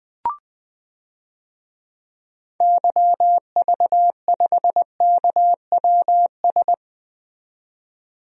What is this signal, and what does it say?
Morse code, heard as a keyed tone, about 20 wpm, 700 Hz. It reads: YV5KWS
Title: DXCC-100 Call Signs - 6 Characters - 20wpm